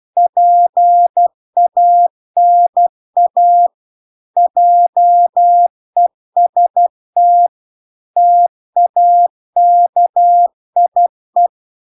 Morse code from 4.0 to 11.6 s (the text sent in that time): JEST TAKIE